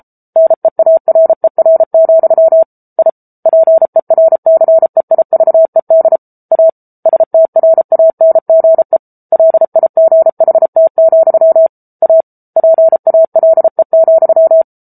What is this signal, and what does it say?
Morse code, heard as a keyed tone, about 33 wpm, 650 Hz. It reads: NEARER, I PERCEIVED A STRANGE LIGHT, A PALE,
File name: Book Two / 07-Chapter 7 - 33 WPM